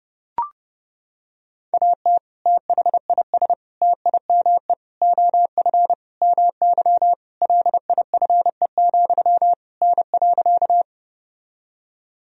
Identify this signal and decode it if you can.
Morse code, heard as a keyed tone, about 32 wpm, 700 Hz. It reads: AT THIS TIME OF MY LIFE, N.
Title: Sentences from Top 300 Words and Abbreviations 30wpm